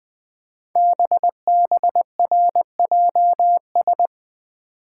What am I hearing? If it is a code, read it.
Morse code, heard as a keyed tone, about 20 wpm, 700 Hz. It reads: BBRJS
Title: Code Groups - 5 Letters - 20wpm